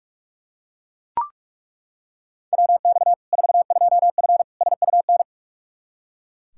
Morse code, heard as a keyed tone, about 41 wpm, 700 Hz. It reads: WX 42F SUN
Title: K1USN SST QSO Practice 45wpm